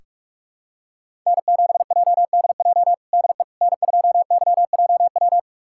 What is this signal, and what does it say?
Morse code, heard as a keyed tone, about 48 wpm, 700 Hz. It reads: N7JDJ DE N2YJW
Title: USA Callsign DE Callsign 45wpm